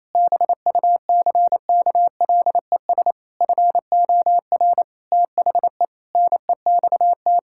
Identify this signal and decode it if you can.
Morse code, heard as a keyed tone, about 28 wpm, 700 Hz. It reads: BUCKLES FOR THE NEXT
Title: Pangram Challenge v2 - 28wpm